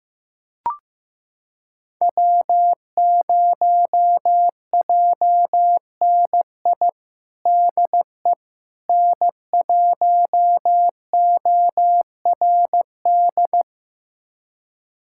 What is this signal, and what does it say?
Morse code, heard as a keyed tone, about 15 wpm, 700 Hz. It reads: W0JNI DE N1ORD
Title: USA Callsign DE Callsign - 5 Characters - 15wpm